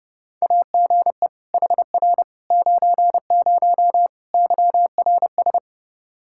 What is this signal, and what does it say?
Morse code, heard as a keyed tone, about 32 wpm, 700 Hz. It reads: AGE HR 90 YRS
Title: Practice QSO - Exchange 3 - More Info - 30wpm